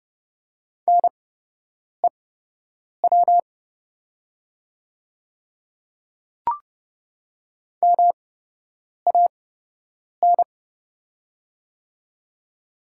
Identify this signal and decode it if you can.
Morse code, heard as a keyed tone, about 31 wpm, spaced as for 8 wpm, 700 Hz. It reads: NEW MAN
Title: 8x Character Spacing - Top 300 Words - 30wpm